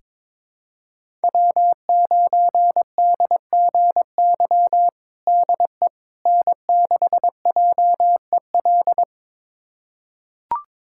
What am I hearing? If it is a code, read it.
Morse code, heard as a keyed tone, about 22 wpm, 700 Hz. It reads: W9DGY DE N6JEL
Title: USA Callsign DE Callsign - 5 Characters - 22wpm